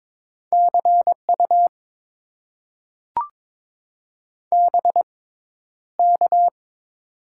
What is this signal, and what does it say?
Morse code, heard as a keyed tone, about 22 wpm, spaced as for 11 wpm, 700 Hz. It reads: CU BK
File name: Top 90 QSO Elements - 6x Character Spacing - 22wpm